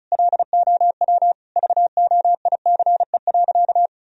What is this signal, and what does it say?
Morse code, heard as a keyed tone, about 36 wpm, 700 Hz. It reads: LOW VOICE.
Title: Sentences from Top 1000 Words - Review Part 1 - 35wpm